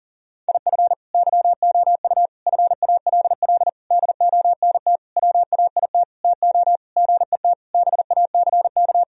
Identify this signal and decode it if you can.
Morse code, heard as a keyed tone, about 40 wpm, 700 Hz. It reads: IF YOU FALL DONT WAIT TO GET BACK
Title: Sentences from Top 1000 Words - Review Part 2 - 40wpm